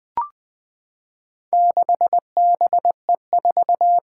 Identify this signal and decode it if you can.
Morse code, heard as a keyed tone, about 21 wpm, 700 Hz. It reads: 6BE4
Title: Four Challenging Letter-Number Combinations 20wpm